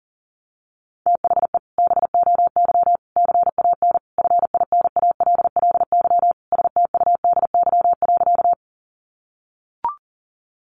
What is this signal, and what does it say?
Morse code, heard as a keyed tone, about 40 wpm, 700 Hz. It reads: THE BOY CAN FINALLY STUDY.
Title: Additional Easy Phrases 40wpm